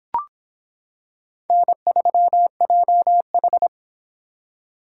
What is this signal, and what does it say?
Morse code, heard as a keyed tone, about 26 wpm, 700 Hz. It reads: N3JH